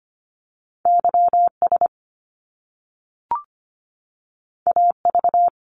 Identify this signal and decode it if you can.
Morse code, heard as a keyed tone, about 25 wpm, 700 Hz. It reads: YS AV
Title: Top Two Letter Combinations in Words 25wpm